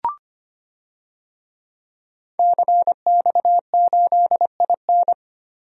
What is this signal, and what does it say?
Morse code, heard as a keyed tone, about 25 wpm, 700 Hz. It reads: CX8IN
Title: DXCC-100 Call Signs - 5 Characters - Repeated 2x - 25wpm